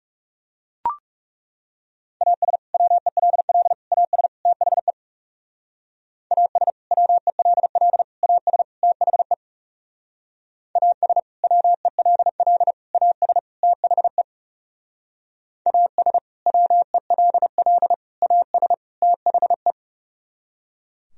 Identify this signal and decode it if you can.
Morse code, heard as a keyed tone, about 37 wpm, 700 Hz. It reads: AS WELL AS THE AS WELL AS THE AS WELL AS THE AS WELL AS THE
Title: Speed-Racing - Sets of 4 Words 30wpm